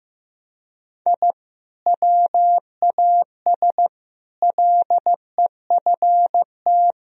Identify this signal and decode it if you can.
Morse code, heard as a keyed tone, about 16 wpm, 700 Hz. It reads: I WAS LEFT
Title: Sentences from Top 1000 Words - Review Part 1 - 15wpm